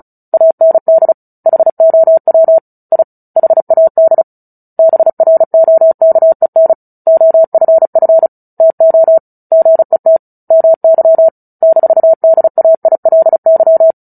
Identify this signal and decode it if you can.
Morse code, heard as a keyed tone, about 34 wpm, 650 Hz. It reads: AND HOW I HAD BROKEN OFF TO GET MY -DAILY
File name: Book Two / 09-Chapter 9 - 35 WPM